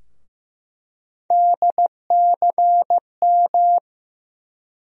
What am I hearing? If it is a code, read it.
Morse code, heard as a keyed tone, about 15 wpm, 700 Hz. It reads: DCM
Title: USA Call Sign Postfixes - 3 Letters 15wpm